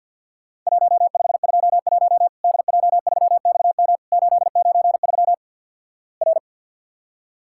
Morse code, heard as a keyed tone, about 50 wpm, 700 Hz. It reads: 1521 DJ3XM 893 R
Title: Worked All Europe DX Contest - QTC Exchange - 50wpm